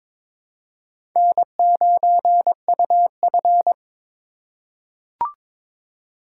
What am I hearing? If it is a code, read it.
Morse code, heard as a keyed tone, about 22 wpm, 700 Hz. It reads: N9UF
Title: USA Call Signs - 4 Characters - 2x Character Spacing - 22wpm